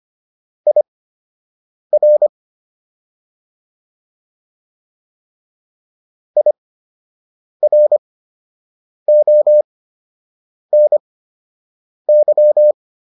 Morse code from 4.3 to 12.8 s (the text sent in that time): IRONY